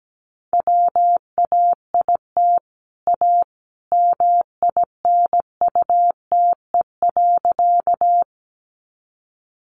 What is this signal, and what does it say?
Morse code, heard as a keyed tone, about 17 wpm, 700 Hz. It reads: WAIT A MINUTE.
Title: Sentences from Top 500 Words 17wpm